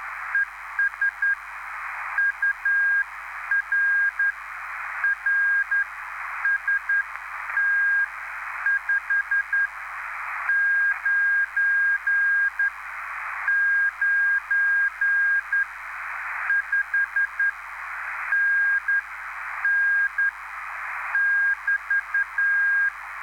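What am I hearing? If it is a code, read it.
Morse code, heard as a keyed tone, about 10 wpm, spaced as for 8 wpm, 1600 Hz, with noise. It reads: ES UR RST 599 5NN <BT>
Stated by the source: Live CW QSO between PA2BX and SP3JFK on the 14055 kHz QRS frequency, at a basic 8 wpm.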